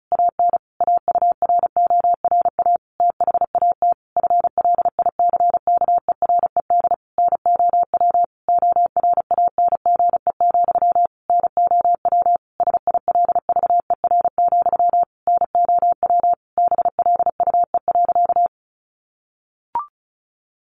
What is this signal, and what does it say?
Morse code, heard as a keyed tone, about 34 wpm, 700 Hz. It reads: AN AURORA THAT FLICKERED NOW ORANGE, NOW SILVER, NOW BLUE.
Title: A Sound Of Thunder 35wpm